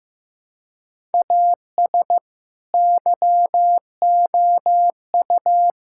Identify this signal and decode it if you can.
Morse code, heard as a keyed tone, about 15 wpm, 700 Hz. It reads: AS YOU
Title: Sentences from Top 1000 Words - Review Part 6 - 15wpm